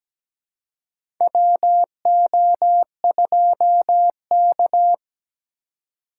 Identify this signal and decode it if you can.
Morse code, heard as a keyed tone, about 17 wpm, 700 Hz. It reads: WO2K
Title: USA Call Signs - 4 Characters - 2x Character Spacing - 17wpm